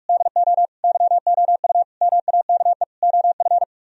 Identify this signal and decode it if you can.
Morse code, heard as a keyed tone, about 48 wpm, 700 Hz. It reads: DO YOU MAKE OF